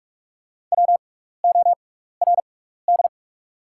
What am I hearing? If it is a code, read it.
Morse code, heard as a keyed tone, about 44 wpm, spaced as for 15 wpm, 700 Hz. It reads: WORD
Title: 6x Character Spacing - Top 300 Words - 45wpm